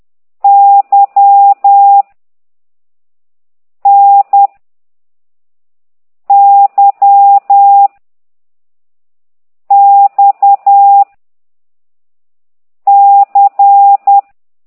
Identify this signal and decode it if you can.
Morse code, heard as a keyed tone, about 10 wpm, 800 Hz. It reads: Y N Y X C
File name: ct2idw / exerc4